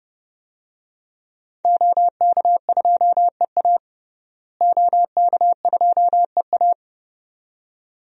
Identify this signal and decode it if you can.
Morse code, heard as a keyed tone, about 30 wpm, 700 Hz. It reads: OK2EA OK2EA
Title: DXCC-100 Call Signs - 5 Characters - Repeated 2x - 30wpm